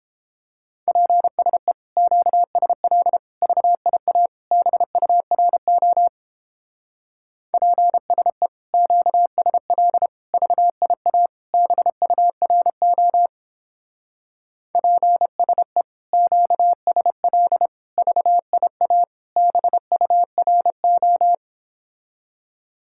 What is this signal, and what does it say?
Morse code, heard as a keyed tone, about 29 wpm, 700 Hz. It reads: PSE QSL VIA BURO PSE QSL VIA BURO PSE QSL VIA BURO